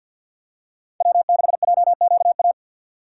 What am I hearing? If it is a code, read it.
Morse code, heard as a keyed tone, about 47 wpm, 700 Hz. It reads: W6JQA
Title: USA Call Signs - 5 Characters - 3x Character Spacing - 50wpm